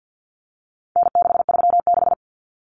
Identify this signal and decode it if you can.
Morse code, heard as a keyed tone, about 50 wpm, 700 Hz. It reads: N636
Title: Common Cut Numbers - 4-Digit 50wpm